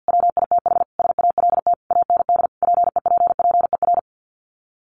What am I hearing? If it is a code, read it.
Morse code, heard as a keyed tone, about 50 wpm, 700 Hz. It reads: WITH SALT AND PEPPER
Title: Sets of 5 Words 50wpm